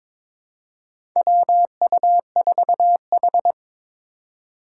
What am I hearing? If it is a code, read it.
Morse code, heard as a keyed tone, about 22 wpm, 700 Hz. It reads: WU4H
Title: USA Call Signs - 3x Character Spacing - 22wpm